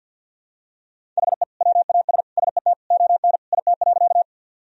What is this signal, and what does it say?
Morse code, heard as a keyed tone, about 46 wpm, 700 Hz. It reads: HE WAS SET ON IT.
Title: Sentences from Top 1000 Words - Review Part 1 - 50wpm